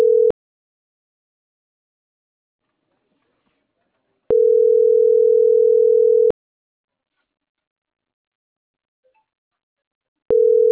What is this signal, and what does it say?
Ringback tone (North America), 440 + 480 Hz.